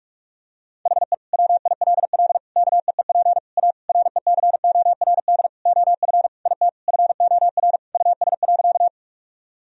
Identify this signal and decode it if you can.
Morse code, heard as a keyed tone, about 46 wpm, 700 Hz. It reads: HE WILL KEEP A RECORD OF IT FOR US.
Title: Sentences from Top 1000 Words - Review Part 3 - 45wpm